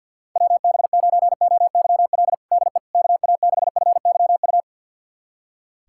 Morse code, heard as a keyed tone, about 47 wpm, 700 Hz. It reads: WB9OYL DE KA6FYU
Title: USA Callsign DE Callsign - 6 Characters - 50wpm